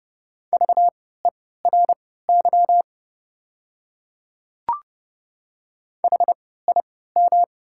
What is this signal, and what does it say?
Morse code, heard as a keyed tone, about 30 wpm, spaced as for 17 wpm, 690 Hz. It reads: VERY HIM
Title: 3x Character Spacing - Top 200 Words - 30wpm